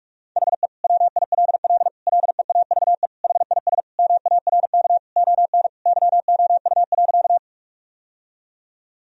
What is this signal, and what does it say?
Morse code, heard as a keyed tone, about 46 wpm, 700 Hz. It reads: HE WILL LEAVE HIS MARK ON YOU.
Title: Sentences from Top 1000 Words - Review Part 1 - 45wpm